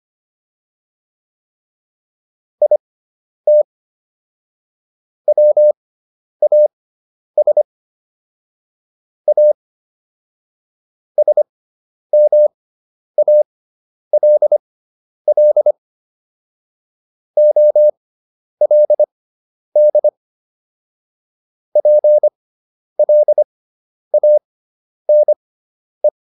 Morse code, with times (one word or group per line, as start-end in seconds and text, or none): 2.6-3.6 IT
5.3-7.6 WAS
9.3-9.5 A
11.2-15.7 SMALL
17.4-20.1 OLD
21.8-26.1 PLANE